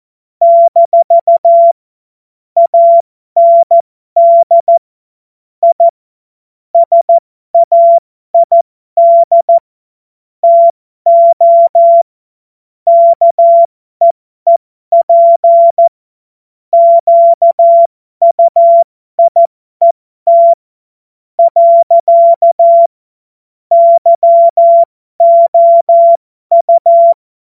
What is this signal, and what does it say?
Morse code, heard as a keyed tone, about 14 wpm, spaced as for 12 wpm, 680 Hz. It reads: - AND I SAID TO KEEP QUIET . YOU